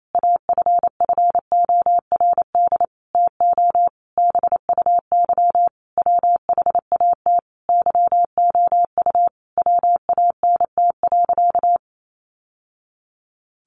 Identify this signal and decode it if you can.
Morse code, heard as a keyed tone, about 28 wpm, 700 Hz. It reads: AFFORD TO BUY WHAT YOU WANT.
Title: Sentences from Top 1100 Words 28wpm